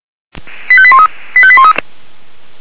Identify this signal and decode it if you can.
Five-tone selective call (ZVEI), heard as quick runs of five tones, tones of about 70 ms. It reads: 85813 85813